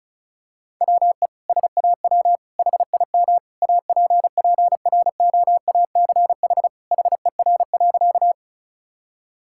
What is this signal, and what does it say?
Morse code, heard as a keyed tone, about 36 wpm, 700 Hz. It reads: WE SAW HIM APPROACH HER.
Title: Sentences from Top 1000 Words - Review Part 6 - 35wpm